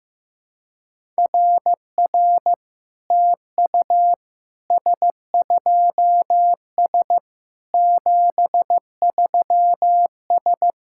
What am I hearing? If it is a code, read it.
Morse code, heard as a keyed tone, about 15 wpm, 700 Hz. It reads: RR TU S2S 73S
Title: SOTA QSO Practice 15wpm